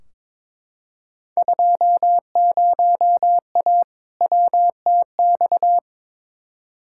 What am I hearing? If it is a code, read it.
Morse code, heard as a keyed tone, about 22 wpm, 700 Hz. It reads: 20A WTX
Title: Field Day Exchanges Encore - 22wpm